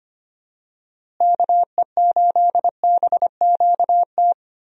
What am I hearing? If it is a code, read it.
Morse code, heard as a keyed tone, about 25 wpm, 700 Hz. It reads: KE8BQT
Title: USA Call Signs - 6 Characters - Repeated 2x - 25wpm